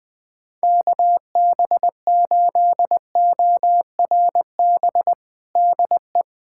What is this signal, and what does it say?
Morse code, heard as a keyed tone, about 20 wpm, 700 Hz. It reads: KB8ORB DE
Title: USA Callsign DE Callsign - 6 Characters - 20wpm